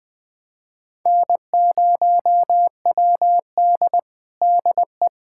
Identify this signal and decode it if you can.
Morse code, heard as a keyed tone, about 20 wpm, 700 Hz. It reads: N0WD DE
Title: USA Callsign DE Callsign - 4 Characters - 20wpm